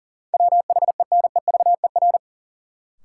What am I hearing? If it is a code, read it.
Morse code, heard as a keyed tone, about 40 wpm, 700 Hz. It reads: WHENEVER
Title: Sentences from Top 1200 Words - New Word Drill Alt - 40wpm